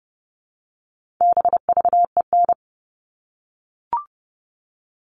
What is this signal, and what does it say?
Morse code, heard as a keyed tone, about 30 wpm, 700 Hz. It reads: BVEN
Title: All Cut Numbers - 4-Digit 30wpm